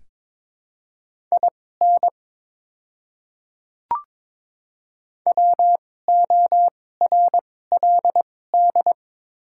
Morse code, heard as a keyed tone, about 21 wpm, spaced as for 16 wpm, 700 Hz. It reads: IN WORLD
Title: 2x Character Spacing - Top 300 Words - 22wpm